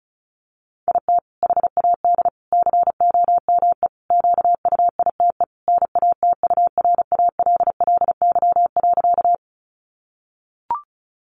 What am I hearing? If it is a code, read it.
Morse code, heard as a keyed tone, about 34 wpm, 700 Hz. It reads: IT HAD COME QUITE NATURALLY.
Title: Sentences from Top 1300 Words 35wpm